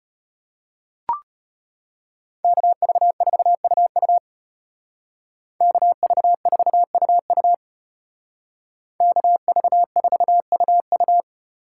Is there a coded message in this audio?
Yes: Morse code from 2.4 to 11.2 s, heard as a keyed tone, about 32 wpm, 700 Hz. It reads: KV4UU KV4UU KV4UU